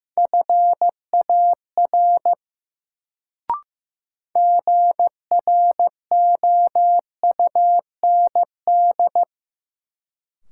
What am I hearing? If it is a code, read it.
Morse code, heard as a keyed tone, about 15 wpm, 700 Hz. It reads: FAR GROUND
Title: Sets of 3 Words - New Word Drill - 15wpm